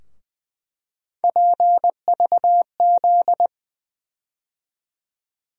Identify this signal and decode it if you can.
Morse code, heard as a keyed tone, about 20 wpm, 700 Hz. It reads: PVZ
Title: USA Call Sign Postfixes - 3 Letters 20wpm